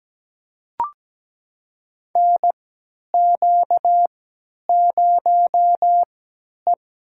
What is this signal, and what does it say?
Morse code, heard as a keyed tone, about 17 wpm, spaced as for 10 wpm, 700 Hz. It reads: NQ0E
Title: USA Call Signs - 4 Characters - 3x Character Spacing - 17wpm